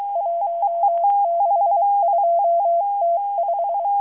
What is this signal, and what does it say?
Morse code, heard as a keyed tone, about 23 wpm, 680 Hz. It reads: 162T5